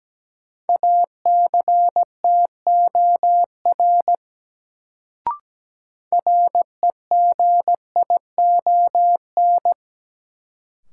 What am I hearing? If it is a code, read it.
Morse code, heard as a keyed tone, about 17 wpm, 700 Hz. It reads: ACTOR REGION
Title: Sentences from Top 1500 Words - New Word Drill - 17wpm